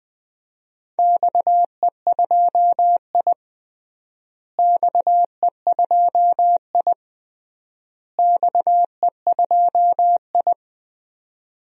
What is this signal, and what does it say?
Morse code, heard as a keyed tone, about 20 wpm, 700 Hz. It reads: XE2I XE2I XE2I
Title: DXCC-100 Call Signs - 4 Characters - Repeated 3x - 20wpm